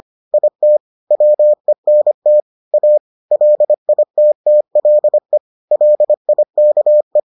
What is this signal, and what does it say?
Morse code, heard as a keyed tone, about 25 wpm, 600 Hz. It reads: IT WENT A LITTLE LIKE